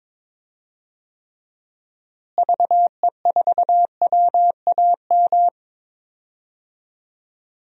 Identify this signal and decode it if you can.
Morse code, heard as a keyed tone, about 22 wpm, 700 Hz. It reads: VE4WAM
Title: DXCC-100 Call Signs - 6 Characters - Repeated 3x - 22wpm